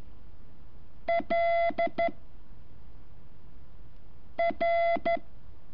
Morse code, heard as a keyed tone, about 13 wpm, 680 Hz. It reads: L R
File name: curso / ejercicio 15